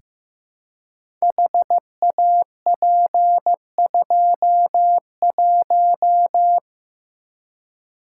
Morse code, heard as a keyed tone, about 15 wpm, 700 Hz. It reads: HAP21
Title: Code Groups - 5 Letter:Numbers - 15wpm